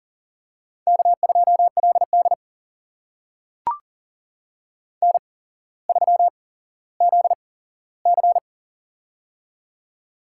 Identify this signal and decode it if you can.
Morse code, heard as a keyed tone, about 39 wpm, spaced as for 17 wpm, 700 Hz. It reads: K2LD N3ZC
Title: USA Call Signs - 4 Characters - 8x Character Spacing - 40wpm